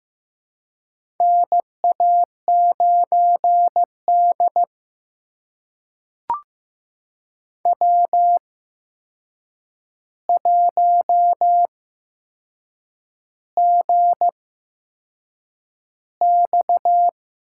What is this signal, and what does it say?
Morse code, heard as a keyed tone, about 15 wpm, spaced as for 6 wpm, 700 Hz. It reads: NA9D W1GX
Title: USA Call Signs - 4 Characters - 8x Character Spacing - 15wpm